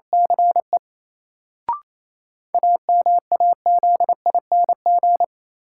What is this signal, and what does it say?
Morse code, heard as a keyed tone, about 28 wpm, 700 Hz. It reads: CE AMAZING
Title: Sentences from Top 1500 Words - New Word Drill - 28wpm